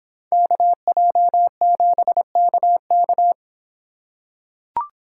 Morse code, heard as a keyed tone, about 26 wpm, 700 Hz. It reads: KJ7KK